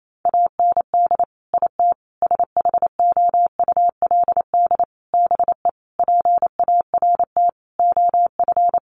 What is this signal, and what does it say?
Morse code, heard as a keyed tone, about 28 wpm, 700 Hz. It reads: AND IT SHOULD BE PART OF
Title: Sentences from Top 1500 Words 28wpm